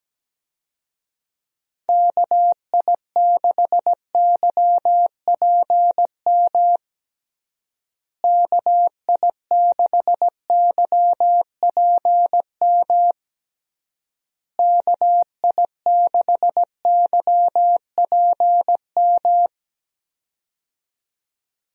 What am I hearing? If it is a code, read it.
Morse code, heard as a keyed tone, about 17 wpm, 700 Hz. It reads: KI6YPM KI6YPM KI6YPM
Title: USA Call Signs - 6 Characters - Repeated 3x - 17wpm